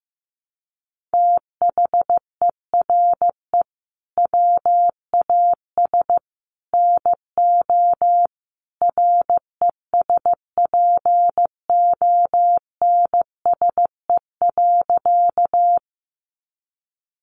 Morse code, heard as a keyed tone, about 15 wpm, 700 Hz. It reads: THERE WAS NO RESPONSE.
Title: Sentences from Top 1400 Words 15wpm